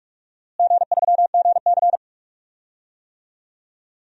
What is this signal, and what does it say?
Morse code, heard as a keyed tone, about 46 wpm, 700 Hz. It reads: G3GC
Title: DXCC-100 Call Signs - 4 Characters - Repeated 2x - 45wpm